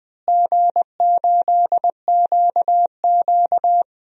Morse code, heard as a keyed tone, about 20 wpm, 700 Hz. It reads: G8QQ
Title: DXCC-100 Call Signs - 4 Characters - Repeated 2x - 20wpm